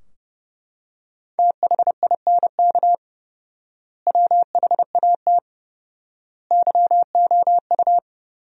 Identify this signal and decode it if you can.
Morse code, heard as a keyed tone, about 30 wpm, 700 Hz. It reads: THINK WHAT YOU
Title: 4x Word-Spacing - Sentences from Top 200 Words - 30wpm